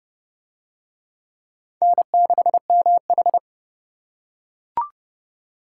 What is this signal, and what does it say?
Morse code, heard as a keyed tone, about 25 wpm, 700 Hz. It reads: N6MH
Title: Speed Racing - USA 1x2 Call Signs 20wpm